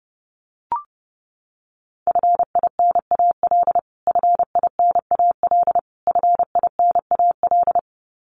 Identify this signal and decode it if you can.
Morse code, heard as a keyed tone, about 30 wpm, 700 Hz. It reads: FINAL FINAL FINAL
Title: Top 500 Words Repeated 30wpm